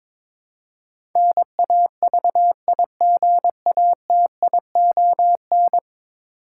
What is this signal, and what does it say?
Morse code, heard as a keyed tone, about 21 wpm, 700 Hz. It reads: NAVIGATION
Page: morse-code-ninja-practice-22wpm